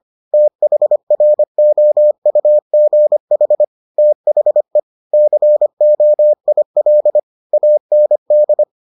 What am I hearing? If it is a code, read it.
Morse code, heard as a keyed tone, about 25 wpm, 600 Hz. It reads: THROUGH THE COIL AND